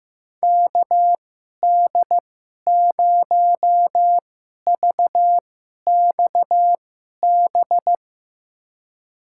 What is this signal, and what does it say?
Morse code, heard as a keyed tone, about 15 wpm, spaced as for 11 wpm, 700 Hz. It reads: KD0VXB
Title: USA Call Signs - 6 Characters - 2x Character Spacing - 15wpm